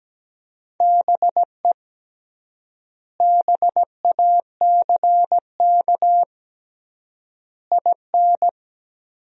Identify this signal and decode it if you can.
Morse code, heard as a keyed tone, about 17 wpm, 700 Hz. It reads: BE BACK IN
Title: 3x Word-Spacing - Sentences from Top 500 Words - 17wpm